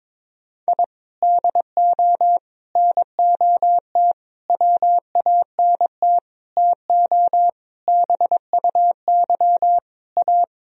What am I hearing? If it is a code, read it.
Morse code, heard as a keyed tone, about 22 wpm, 700 Hz. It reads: I DO NOT WANT TO BUY A
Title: Sentences from Top 1000 Words - Review Part 4 - 22wpm